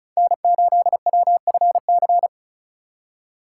Morse code, heard as a keyed tone, about 36 wpm, 700 Hz. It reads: N8WFC
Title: USA Call Signs - 5 Characters - 2x Character Spacing - 35wpm